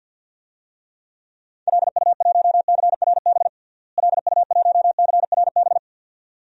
Morse code, heard as a keyed tone, about 52 wpm, 700 Hz. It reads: LU1CRB LU1CRB
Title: DXCC-100 Call Signs - 6 Characters - Repeated 2x - 50wpm